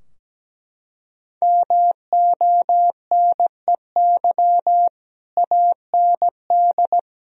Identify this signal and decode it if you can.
Morse code, heard as a keyed tone, about 17 wpm, 700 Hz. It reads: MONEY AND
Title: Sentences from Top 1000 Words - Review Part 4 - 17wpm